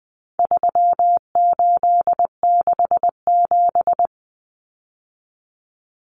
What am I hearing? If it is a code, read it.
Morse code, heard as a keyed tone, about 20 wpm, 700 Hz. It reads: 3867
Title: Numbers - 4-Digit 20wpm